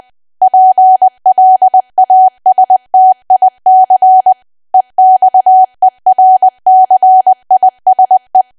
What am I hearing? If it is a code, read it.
Morse code, heard as a keyed tone, about 20 wpm, 750 Hz. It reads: PLASTIC EXERCISE
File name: cwa / W1AW-Level3 / 20WPM